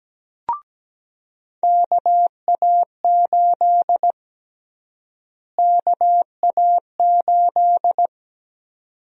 Morse code, heard as a keyed tone, about 17 wpm, 700 Hz. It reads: KA8 KA8
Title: USA Call Sign Prefixes - Repeated 2x - 17wpm